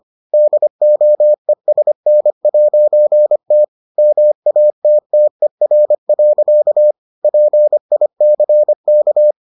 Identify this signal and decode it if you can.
Morse code, heard as a keyed tone, about 25 wpm, 600 Hz. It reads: DOESN'T MATTER. PICK